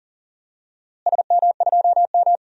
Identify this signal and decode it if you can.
Morse code, heard as a keyed tone, about 40 wpm, 700 Hz. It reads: SM2M
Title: DXCC-100 Call Signs - 4 Characters - Repeated 2x - 40wpm